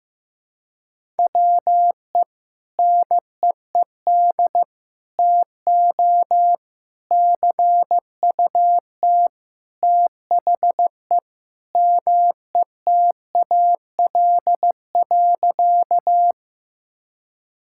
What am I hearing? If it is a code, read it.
Morse code, heard as a keyed tone, about 15 wpm, 700 Hz. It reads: WE NEED TO CUT THE METAL.
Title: Sentences from Top 1000 Words - Review Part 6 - 15wpm